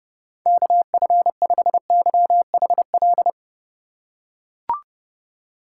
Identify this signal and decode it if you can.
Morse code, heard as a keyed tone, about 30 wpm, 700 Hz. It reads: KF5YHL
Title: USA Call Signs - 6 Characters - 4x Character Spacing - 30wpm